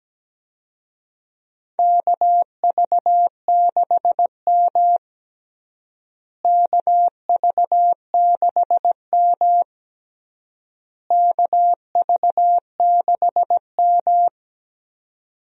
Morse code, heard as a keyed tone, about 17 wpm, 700 Hz. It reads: KV6M KV6M KV6M
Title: USA Call Signs - 4 Characters - Repeated 3x - 17wpm